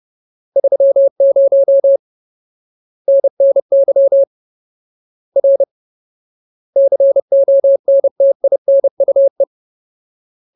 Morse code, heard as a keyed tone, about 30 wpm, 550 Hz. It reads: 30 NNY R CONTINUE
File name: Ultimate Word List / 30WPM Ultimate